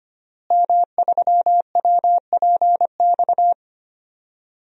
Morse code, heard as a keyed tone, about 25 wpm, 700 Hz. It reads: M3WPX
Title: DXCC-100 Call Signs - 5 Characters - Repeated 3x - 25wpm